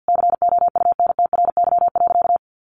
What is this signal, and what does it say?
Morse code, heard as a keyed tone, about 50 wpm, 700 Hz. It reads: COUNTRY.
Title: Sentences from Top 300 Words 50wpm